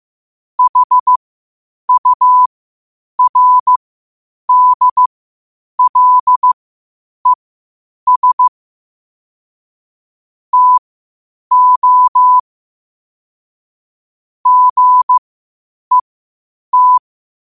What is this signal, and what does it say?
Morse code, heard as a keyed tone, about 15 wpm, spaced as for 8 wpm, 1000 Hz. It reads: HURDLES TO GET
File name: qotdfiles / 8wpm